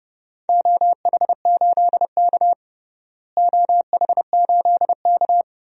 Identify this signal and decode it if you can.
Morse code, heard as a keyed tone, about 30 wpm, 700 Hz. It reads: OH8K OH8K
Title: DXCC-100 Call Signs - 4 Characters - Repeated 2x - 30wpm